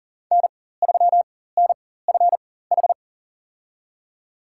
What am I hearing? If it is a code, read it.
Morse code, heard as a keyed tone, about 39 wpm, spaced as for 19 wpm, 700 Hz. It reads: N3NFH
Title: USA Call Signs - 5 Characters - 4x Character Spacing - 40wpm